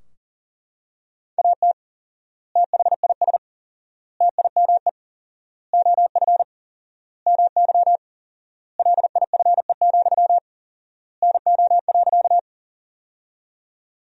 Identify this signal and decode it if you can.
Morse code, heard as a keyed tone, about 40 wpm, 700 Hz. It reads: AT THIS TIME OF MY LIFE, NO.
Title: 4x Word-Spacing - Sentences from Top 300 Words - 40wpm